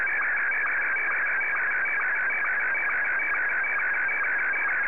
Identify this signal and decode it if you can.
SSTV (slow-scan television), Martin 1.
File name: sstv